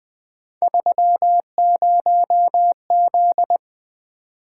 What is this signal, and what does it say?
Morse code, heard as a keyed tone, about 20 wpm, 700 Hz. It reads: 30Z
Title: Three Letter-Number Combinations - 20wpm